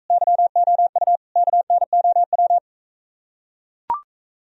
Morse code, heard as a keyed tone, about 43 wpm, 700 Hz. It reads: YOU KNOW